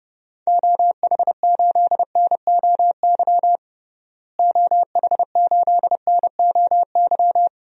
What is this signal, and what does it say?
Morse code, heard as a keyed tone, about 30 wpm, 700 Hz. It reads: OH8NOY OH8NOY
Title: DXCC-100 Call Signs - 6 Characters - Repeated 3x - 30wpm